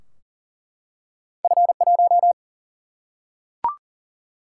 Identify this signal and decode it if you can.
Morse code, heard as a keyed tone, about 40 wpm, 700 Hz. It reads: F 1
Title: Single-Letter Single-Number 40wpm